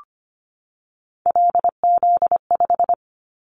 Morse code, heard as a keyed tone, about 25 wpm, 700 Hz. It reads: LZ5
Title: Callsign Prefixes for Top 50 DXCC - 25wpm